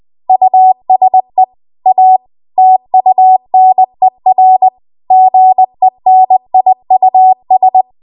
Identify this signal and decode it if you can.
Morse code, heard as a keyed tone, about 20 wpm, 750 Hz. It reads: USE A TUNER GENIUS
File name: Archive / 20 WPM / 20WPM